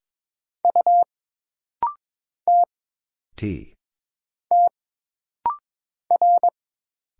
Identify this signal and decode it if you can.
Morse code, heard as a keyed tone, about 22 wpm, 700 Hz. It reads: U T T R
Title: Single Letter-Number Alt - Rapid-Fire 22wpm